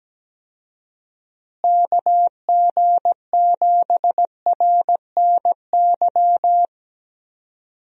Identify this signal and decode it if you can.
Morse code, heard as a keyed tone, about 17 wpm, 700 Hz. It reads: KG7RNY
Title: USA Call Signs - 6 Characters - Repeated 3x - 17wpm